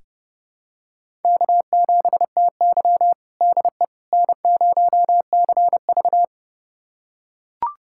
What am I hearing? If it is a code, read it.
Morse code, heard as a keyed tone, about 32 wpm, 700 Hz. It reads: K7TY DE N0CV
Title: USA Callsign DE Callsign - 4 Characters - 30wpm